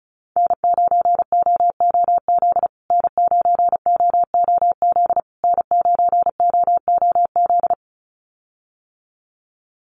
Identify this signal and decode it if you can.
Morse code, heard as a keyed tone, about 34 wpm, 700 Hz. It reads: N9OOZ N9OOZ N9OOZ
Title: USA Call Signs Repeated 3x 35wpm